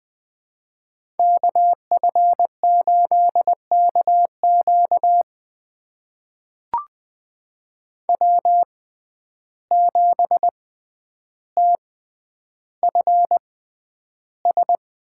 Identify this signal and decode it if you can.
Morse code, heard as a keyed tone, about 20 wpm, spaced as for 10 wpm, 700 Hz. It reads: KF8KQ W7TFS
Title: USA Call Signs - 5 Characters - 6x Character Spacing - 20wpm